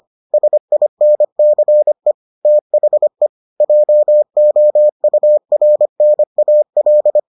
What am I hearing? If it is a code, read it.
Morse code, heard as a keyed tone, about 25 wpm, 600 Hz. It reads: SINCE THE JOURNAL